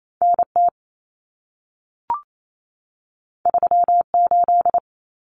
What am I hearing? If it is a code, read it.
Morse code, heard as a keyed tone, about 28 wpm, 700 Hz. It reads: NT 38
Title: Common Cut Numbers - 2-Digit 28wpm